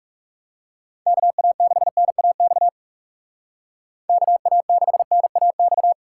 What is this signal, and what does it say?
Morse code, heard as a keyed tone, about 39 wpm, 700 Hz. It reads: KA6NAX KA6NAX